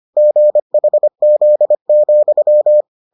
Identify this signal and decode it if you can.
Morse code, heard as a keyed tone, about 25 wpm, 600 Hz. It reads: GHZ,